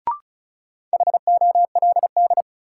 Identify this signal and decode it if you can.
Morse code, heard as a keyed tone, about 36 wpm, 700 Hz. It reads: HOLD
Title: Sentences from Top 400 Words - New Word Drill - 35wpm